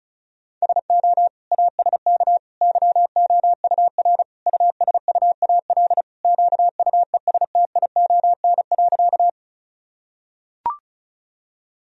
Morse code, heard as a keyed tone, about 34 wpm, 700 Hz. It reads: SO ASK YOUR USUAL QUESTION.
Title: Sentences from Top 1000 Words - Review Part 2 - 35wpm